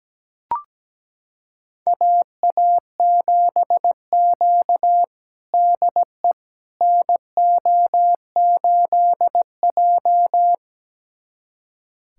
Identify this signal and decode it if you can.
Morse code, heard as a keyed tone, about 17 wpm, 700 Hz. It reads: AA7Q DE NO8J
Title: USA Callsign DE Callsign - 4 Characters - 17wpm